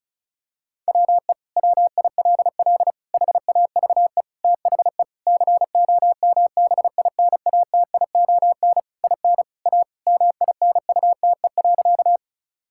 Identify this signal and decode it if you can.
Morse code, heard as a keyed tone, about 34 wpm, 700 Hz. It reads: WE WILL HAVE THE COMBINATION IN A MINUTE.
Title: Sentences from Top 1000 Words - Review Part 6 - 35wpm